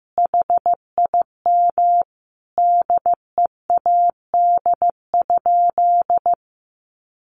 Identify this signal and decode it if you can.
Morse code, heard as a keyed tone, about 15 wpm, 700 Hz. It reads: HIM DEAD?
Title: Sentences from Top 1500 Words 15wpm